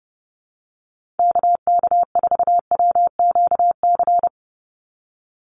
Morse code, heard as a keyed tone, about 30 wpm, 700 Hz. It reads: KK4WQC
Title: USA Call Signs 30wpm